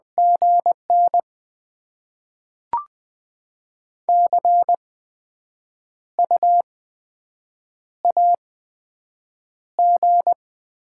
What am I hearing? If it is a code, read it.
Morse code, heard as a keyed tone, about 20 wpm, 700 Hz. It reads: G N C U A G
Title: Top 20 QSO Elements - 8x Character Spacing - 20wpm